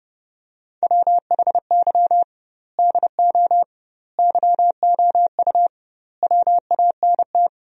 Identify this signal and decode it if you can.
Morse code, heard as a keyed tone, about 30 wpm, 700 Hz. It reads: WHY DO YOU WANT
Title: 2x Word-Spacing - Sentences from Top 200 Words - 30wpm